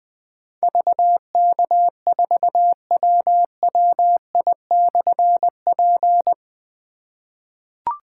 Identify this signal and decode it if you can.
Morse code, heard as a keyed tone, about 20 wpm, 700 Hz. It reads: VK4WWI/P
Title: Difficult Call Signs 20wpm